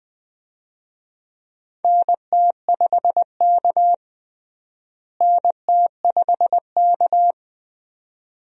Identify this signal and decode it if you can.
Morse code, heard as a keyed tone, about 20 wpm, 700 Hz. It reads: NT5K NT5K
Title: USA Call Signs - 4 Characters - Repeated 2x - 20wpm